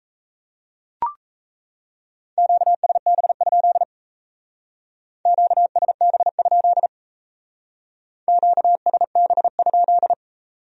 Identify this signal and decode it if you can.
Morse code, heard as a keyed tone, about 35 wpm, 700 Hz. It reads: QSB? QSB? QSB?